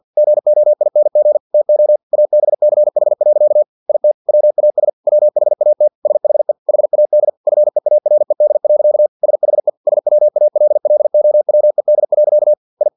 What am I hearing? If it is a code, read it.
Morse code, heard as a keyed tone, about 50 wpm, 600 Hz. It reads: GOING TO ABCH. IT WAS WHAT SHE HAD FEARED. SHE SWALLOWED. I
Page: Never 49wpm